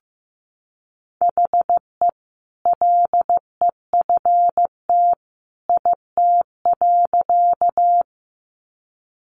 Morse code, heard as a keyed tone, about 15 wpm, 700 Hz. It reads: HE LEFT IT.
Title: A Sound Of Thunder 15wpm